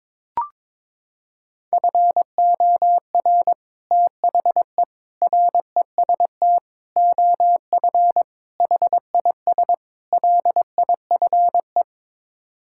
Morse code, heard as a keyed tone, about 22 wpm, 700 Hz. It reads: FOR THE REST OF HIS LIFE
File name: practice / 22wpm / Sets of 6 Words - 22wpm